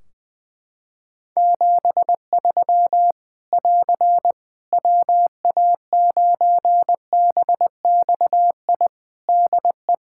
Morse code, heard as a keyed tone, about 21 wpm, 700 Hz. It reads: 73 <AR> WA9BXI DE
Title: Practice QSO - Exchange 4 - Ending - 20wpm